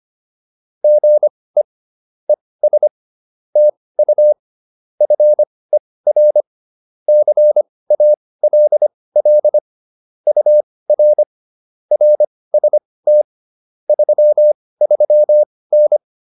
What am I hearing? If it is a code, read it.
Morse code, heard as a keyed tone, about 25 wpm, spaced as for 18 wpm, 600 Hz. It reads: GE ES TU FER CALL UR RST 33N
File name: QSO 105 18